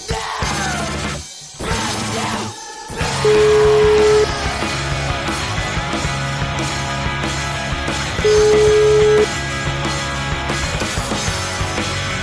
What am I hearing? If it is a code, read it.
Ringback tone (Europe, 425 Hz), with noise.